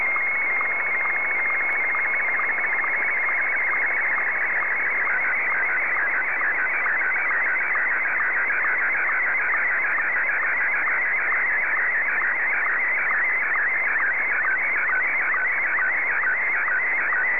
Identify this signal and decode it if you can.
SSTV (slow-scan television), Martin 1.
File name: sstv / M1-VK3DFL